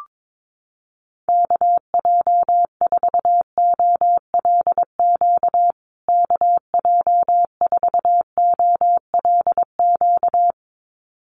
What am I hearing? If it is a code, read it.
Morse code, heard as a keyed tone, about 22 wpm, 700 Hz. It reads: KJ4OLQ KJ4OLQ
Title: USA Call Signs Repeated 2x 22wpm